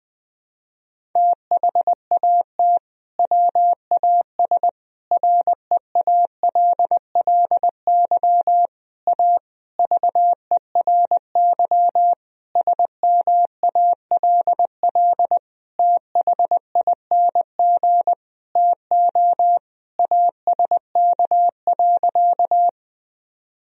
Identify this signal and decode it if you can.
Morse code, heard as a keyed tone, about 20 wpm, 700 Hz. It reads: THAT WAS REALLY A VERY SMALL THING TO ASK.
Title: Sentences from Top 1000 Words - Review Part 1 - 20wpm